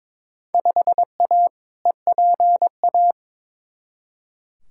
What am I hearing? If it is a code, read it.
Morse code, heard as a keyed tone, about 22 wpm, 700 Hz. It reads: 5A EPA
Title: Field Day Exchanges - 22wpm